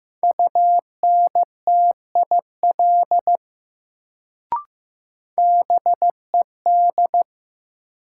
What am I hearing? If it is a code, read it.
Morse code, heard as a keyed tone, about 15 wpm, 700 Hz. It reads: UNTIL BED
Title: Sentences from Top 500 Words - New Word Drill - 15wpm